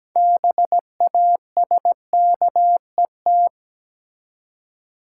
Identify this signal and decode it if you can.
Morse code, heard as a keyed tone, about 17 wpm, 700 Hz. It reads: BASKET
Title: Six Letter Words - 17wpm